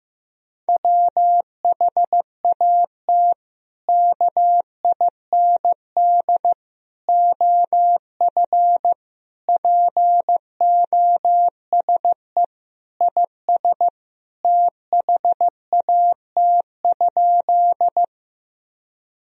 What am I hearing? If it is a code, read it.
Morse code, heard as a keyed tone, about 16 wpm, 700 Hz. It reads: WHAT KIND OF POSE IS THAT?
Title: Sentences from Top 1000 Words - Review Part 2 - 15wpm